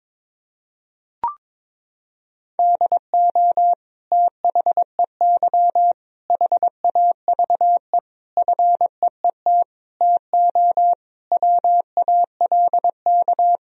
Morse code, heard as a keyed tone, about 22 wpm, 700 Hz. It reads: DO THEY HAVE FEET TO WALK
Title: Sentences from Top 1000 Words - Review Part 1 - 22wpm